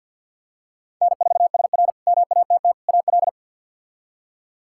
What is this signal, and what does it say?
Morse code, heard as a keyed tone, about 53 wpm, 700 Hz. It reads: N4SR MATT AL
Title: K1USN SST QSO Practice - 50wpm